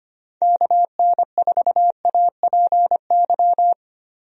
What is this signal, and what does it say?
Morse code, heard as a keyed tone, about 25 wpm, 700 Hz. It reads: KN4APY
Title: USA Call Signs - 6 Characters - 25wpm